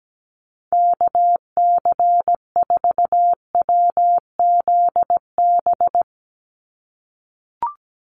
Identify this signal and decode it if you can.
Morse code, heard as a keyed tone, about 17 wpm, 700 Hz. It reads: KC4WZB
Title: NA SOTA Activator Callsigns 17wpm